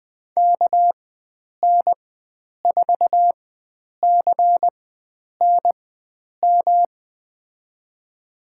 Morse code, heard as a keyed tone, about 20 wpm, spaced as for 9 wpm, 700 Hz. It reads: KN4CNM
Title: USA Call Signs - 4x Character Spacing - 20wpm